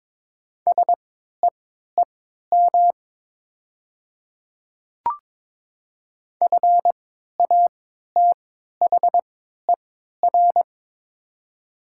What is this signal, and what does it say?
Morse code, heard as a keyed tone, about 22 wpm, spaced as for 13 wpm, 700 Hz. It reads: SEEM FATHER
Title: 3x Character Spacing - Top 300 Words - 22wpm